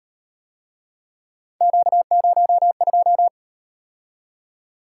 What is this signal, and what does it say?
Morse code, heard as a keyed tone, about 38 wpm, 700 Hz. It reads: Q02